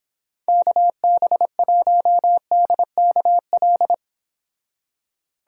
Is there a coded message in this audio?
Yes: Morse code from 0.5 to 3.9 s, heard as a keyed tone, about 26 wpm, 700 Hz. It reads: KB1DKL